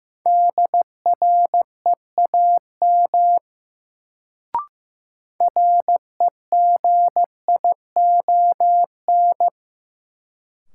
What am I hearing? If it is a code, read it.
Morse code, heard as a keyed tone, about 15 wpm, 700 Hz. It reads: DREAM REGION
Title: Sentences from Top 1500 Words - New Word Drill - 15wpm